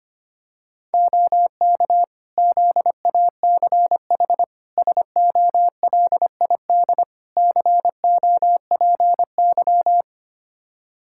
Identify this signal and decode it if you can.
Morse code, heard as a keyed tone, about 25 wpm, 700 Hz. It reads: OK ZACH SOLID COPY
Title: K1USN SST QSO Practice 25wpm